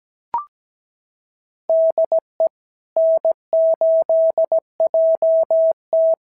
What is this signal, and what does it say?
Morse code, heard as a keyed tone, about 17 wpm, 650 Hz. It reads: DE N8JT
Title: SOTA QSO Practice 17wpm